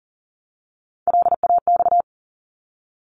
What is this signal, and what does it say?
Morse code, heard as a keyed tone, about 40 wpm, 700 Hz. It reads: LAX
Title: ARRL Sections 40wpm